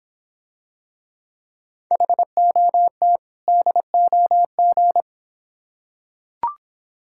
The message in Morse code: HOT DOG

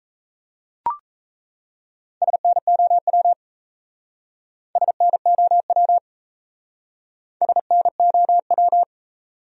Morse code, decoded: SNOW SNOW SNOW

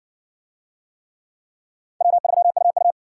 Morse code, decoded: W3UU

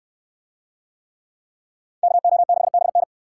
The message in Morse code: KO6BN

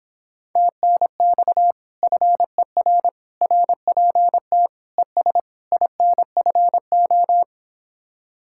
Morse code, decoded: TNX FER RPT ES INFO

